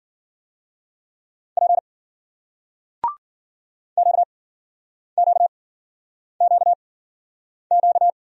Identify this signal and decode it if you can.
Morse code, heard as a keyed tone, about 56 wpm, 700 Hz. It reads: P Q Q Q Q